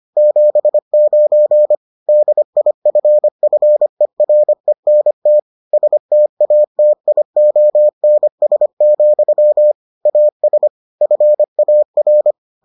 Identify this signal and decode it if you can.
Morse code, heard as a keyed tone, about 25 wpm, 600 Hz. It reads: 79 DIFFERENT STATIONS, AS FAR